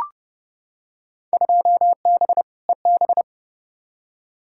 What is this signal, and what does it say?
Morse code, heard as a keyed tone, about 32 wpm, 700 Hz. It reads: 2B EB
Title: Field Day Exchanges - 30wpm